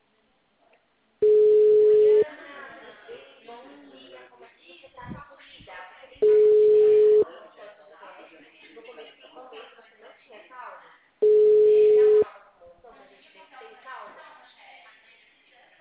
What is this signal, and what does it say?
Ringback tone (Europe, 425 Hz), with speech.
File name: out- -2043- - - .19805